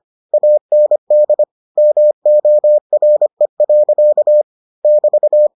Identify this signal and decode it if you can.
Morse code, heard as a keyed tone, about 25 wpm, 600 Hz. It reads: AND MORE. <BT>